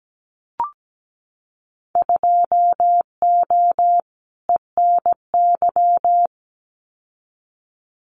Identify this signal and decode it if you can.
Morse code, heard as a keyed tone, about 17 wpm, 700 Hz. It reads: 2O ENY
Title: Winter Field Day Exchanges 17wpm